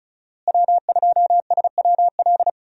Morse code, heard as a keyed tone, about 34 wpm, 700 Hz. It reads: W2SWL
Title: USA Call Signs - 5 Characters - Repeated 2x - 35wpm